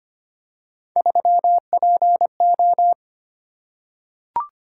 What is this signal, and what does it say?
Morse code, heard as a keyed tone, about 25 wpm, 700 Hz. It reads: 3PO
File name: Three Letter-Number Combinations - 25wpm